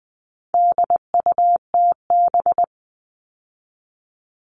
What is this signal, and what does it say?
Morse code, heard as a keyed tone, about 20 wpm, 700 Hz. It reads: DUTB
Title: All Cut Numbers - 4-Digit 20wpm